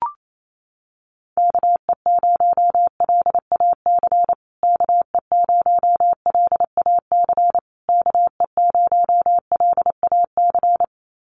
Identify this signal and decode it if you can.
Morse code, heard as a keyed tone, about 28 wpm, 700 Hz. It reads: KE0LAC KE0LAC KE0LAC
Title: USA Call Signs Repeated 3x 28wpm